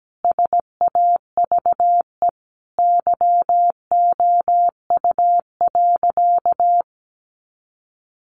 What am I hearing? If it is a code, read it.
Morse code, heard as a keyed tone, about 17 wpm, 700 Hz. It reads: SAVE YOU.
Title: Sentences from Top 1100 Words 17wpm